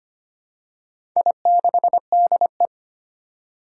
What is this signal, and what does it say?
Morse code, heard as a keyed tone, about 25 wpm, 700 Hz. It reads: I6DE
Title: Four Challenging Letter-Number Combinations 25wpm